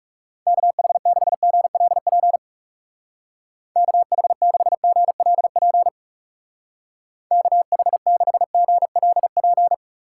Morse code, heard as a keyed tone, about 37 wpm, 700 Hz. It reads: KH6GLP KH6GLP KH6GLP
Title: Speed Racing - USA 2x3 Call Signs 30wpm